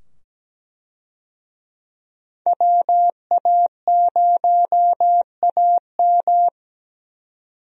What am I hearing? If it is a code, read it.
Morse code, heard as a keyed tone, about 17 wpm, 700 Hz. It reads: WA0AM
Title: USA Call Signs - 5 Characters - 17wpm